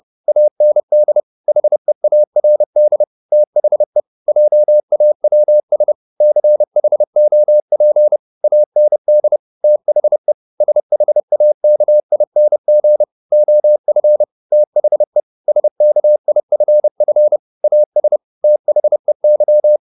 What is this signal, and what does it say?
Morse code, heard as a keyed tone, about 30 wpm, 600 Hz. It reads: AND HEARD THE JAWS CHOP AND THE SHAKING OF THE SKIFF AS THEY